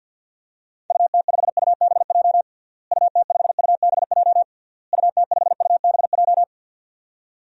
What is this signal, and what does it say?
Morse code, heard as a keyed tone, about 52 wpm, 700 Hz. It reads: UT5UBJ UT5UBJ UT5UBJ
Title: DXCC-100 Call Signs - 6 Characters - Repeated 3x - 50wpm